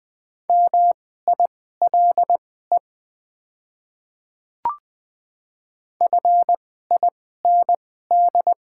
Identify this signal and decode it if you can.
Morse code, heard as a keyed tone, about 20 wpm, spaced as for 14 wpm, 700 Hz. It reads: MILE FIND
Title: 2x Character Spacing - Top 300 Words - 20wpm